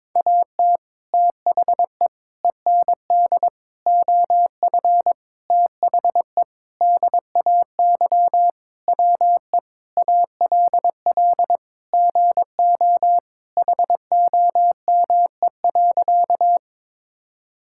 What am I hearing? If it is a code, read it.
Morse code, heard as a keyed tone, about 22 wpm, 700 Hz. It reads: AT THE END OF THE DAY WE ALL GO HOME.
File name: practice / 22wpm / Sentences with 4-Word Phrases - 22wpm